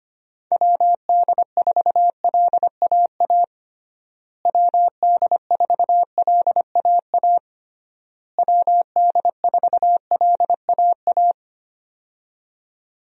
Morse code, heard as a keyed tone, about 25 wpm, 700 Hz. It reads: WD4LAA WD4LAA WD4LAA
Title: USA Call Signs - 6 Characters - Repeated 3x - 25wpm